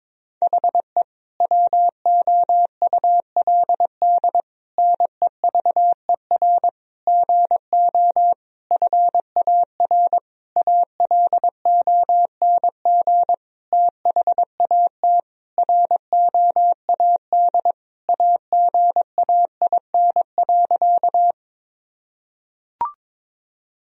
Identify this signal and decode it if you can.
Morse code, heard as a keyed tone, about 22 wpm, 700 Hz. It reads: HE WOULD NEVER GO FAR ALONG THAT ROAD AGAIN.
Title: Sentences from Top 1000 Words - Review Part 2 - 22wpm